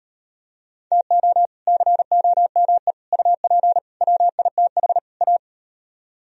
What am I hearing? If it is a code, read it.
Morse code, heard as a keyed tone, about 38 wpm, 700 Hz. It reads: TO COME UP WITH A